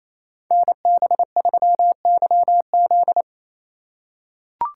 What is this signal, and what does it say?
Morse code, heard as a keyed tone, about 28 wpm, 700 Hz. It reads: NB3YZ
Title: Code Groups - 5 Letter:Numbers - 28wpm